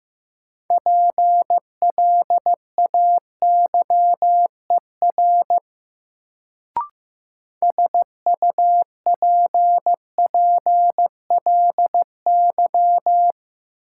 Morse code, heard as a keyed tone, about 15 wpm, 700 Hz. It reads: PLAYER SUPPLY
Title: Sentences from Top 1000 Words - Review Part 4 - New Word Drill - 15wpm